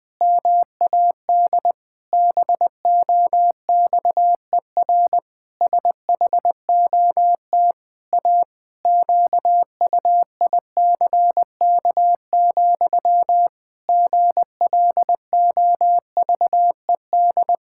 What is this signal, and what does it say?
Morse code, heard as a keyed tone, about 20 wpm, 700 Hz. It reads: MAD BOXER SHOT A QUICK, GLOVED
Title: Pangram Challenge v2 - 20wpm